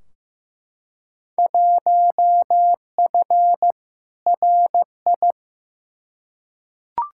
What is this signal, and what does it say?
Morse code, heard as a keyed tone, about 15 wpm, 700 Hz. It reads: 1F RI
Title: Field Day Exchanges Encore - 15wpm